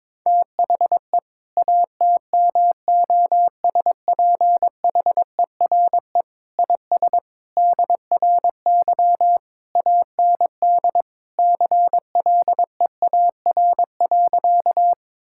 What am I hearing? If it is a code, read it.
Morse code, heard as a keyed tone, about 22 wpm, 700 Hz. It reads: THE ATMOSPHERE IS DRY AND CLEAR.
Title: Sentences from Top 1600 Words 22wpm